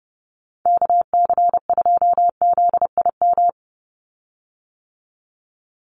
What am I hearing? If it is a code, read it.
Morse code, heard as a keyed tone, about 28 wpm, 700 Hz. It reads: KC2ZIM
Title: USA Call Signs 30wpm